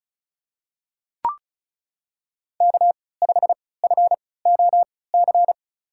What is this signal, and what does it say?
Morse code, heard as a keyed tone, about 36 wpm, spaced as for 20 wpm, 700 Hz. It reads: K5FOC
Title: USA Call Signs - 5 Characters - 3x Character Spacing - 35wpm